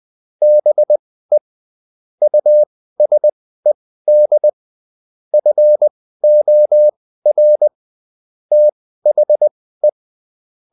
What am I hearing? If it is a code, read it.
Morse code, heard as a keyed tone, about 20 wpm, spaced as for 14 wpm, 600 Hz. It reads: BE USED FOR THE